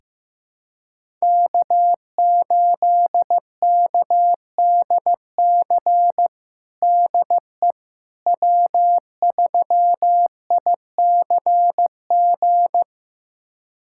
Morse code, heard as a keyed tone, about 15 wpm, 700 Hz. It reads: K8KDC DE W3ICG
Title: USA Callsign DE Callsign - 5 Characters - 15wpm